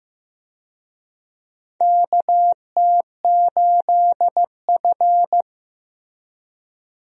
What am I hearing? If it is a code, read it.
Morse code, heard as a keyed tone, about 15 wpm, 700 Hz. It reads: KT8F
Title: USA Call Signs - 4 Characters - Repeated 3x - 15wpm